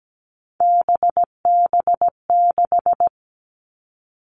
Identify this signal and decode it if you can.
Morse code, heard as a keyed tone, about 17 wpm, 700 Hz. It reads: BB6
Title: Three Challenging Letter-Number Combinations 17wpm